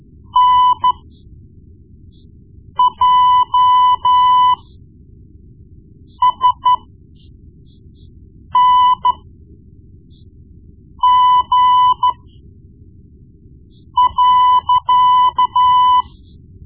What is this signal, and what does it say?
Morse code, heard as a keyed tone, about 13 wpm, spaced as for 4 wpm, 1000 Hz. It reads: NJSNG.
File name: Morse Lessons / Morse Lessons / Tape-1B